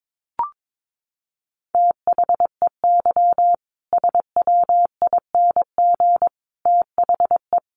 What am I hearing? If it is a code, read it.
Morse code, heard as a keyed tone, about 22 wpm, 700 Hz. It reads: THEY SWING THE
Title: Easy Phrases 22wpm